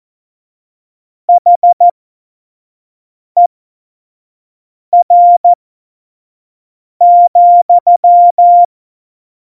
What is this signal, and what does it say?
Morse code, heard as a keyed tone, about 14 wpm, spaced as for 5 wpm, 700 Hz. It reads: HER,
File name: Morsekurs / Witze / Witz 07 April 2020, 25 F70